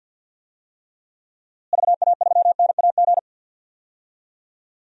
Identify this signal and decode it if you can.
Morse code, heard as a keyed tone, about 47 wpm, 700 Hz. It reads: VA3NAG
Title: DXCC-100 Call Signs - 6 Characters - Repeated 3x - 50wpm